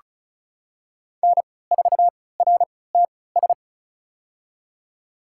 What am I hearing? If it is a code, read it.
Morse code, heard as a keyed tone, about 34 wpm, spaced as for 20 wpm, 700 Hz. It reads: N4RTS
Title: USA Call Signs - 3x Character Spacing - 35wpm